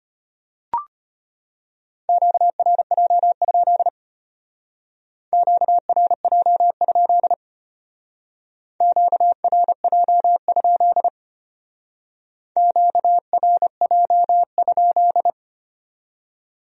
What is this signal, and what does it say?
Morse code, heard as a keyed tone, about 33 wpm, 700 Hz. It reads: QRJ? QRJ? QRJ? QRJ?